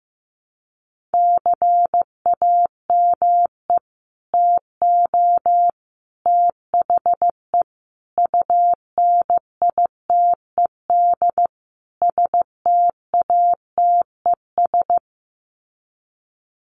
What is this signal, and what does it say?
Morse code, heard as a keyed tone, about 15 wpm, 700 Hz. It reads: CAME TO THE UNITED STATES
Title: Sets of 5 Words Encore2 15wpm